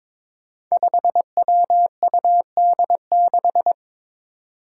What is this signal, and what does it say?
Morse code, heard as a keyed tone, about 22 wpm, 700 Hz. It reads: 5WUD6
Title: Five Challenging Letter-Number Combinations 22wpm